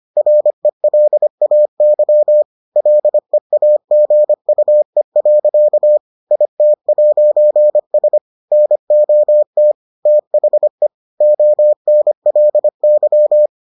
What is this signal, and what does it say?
Morse code, heard as a keyed tone, about 25 wpm, 600 Hz. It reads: RELAY LEAGUE. IT'S NOT THE ONLY